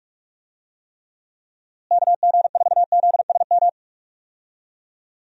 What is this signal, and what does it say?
Morse code, heard as a keyed tone, about 46 wpm, 700 Hz. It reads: KG4ZSM
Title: USA Call Signs - 6 Characters - Repeated 2x - 45wpm